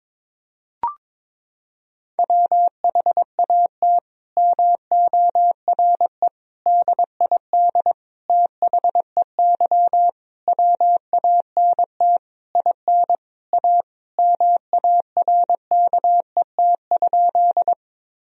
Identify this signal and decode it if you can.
Morse code, heard as a keyed tone, about 22 wpm, 700 Hz. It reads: WHAT MORE DID THEY WANT IN A MARKET?
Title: Sentences from Top 1000 Words - Review Part 3 - 22wpm